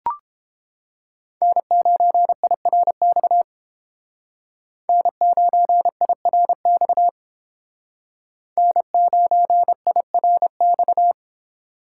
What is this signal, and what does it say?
Morse code, heard as a keyed tone, about 28 wpm, 700 Hz. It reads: N9IRX N9IRX N9IRX